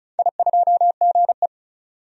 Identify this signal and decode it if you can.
Morse code, heard as a keyed tone, about 36 wpm, 690 Hz. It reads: I2GE
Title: DXCC-100 Call Signs - 4 Characters - 35wpm